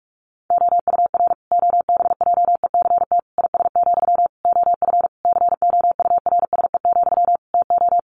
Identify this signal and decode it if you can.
Morse code, heard as a keyed tone, about 44 wpm, 700 Hz. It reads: OUR OBJECT IS, OF COURSE, TO
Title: Sentences from Top 1500 Words 45wpm